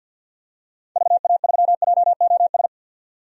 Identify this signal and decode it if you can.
Morse code, heard as a keyed tone, about 51 wpm, 700 Hz. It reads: VA3JOS
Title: DXCC-100 Call Signs - 6 Characters - 50wpm